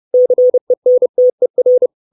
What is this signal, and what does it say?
Morse code, heard as a keyed tone, about 30 wpm, 500 Hz. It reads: CENTER
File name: 632 Words 30WPM